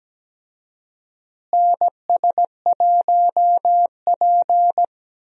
Morse code, heard as a keyed tone, about 17 wpm, 700 Hz. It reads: NS1P
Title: USA Call Signs - 4 Characters - Repeated 3x - 17wpm